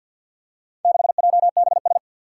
Morse code, heard as a keyed tone, about 51 wpm, 700 Hz. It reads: 6JBS
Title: Four Challenging Letter-Number Combinations 50wpm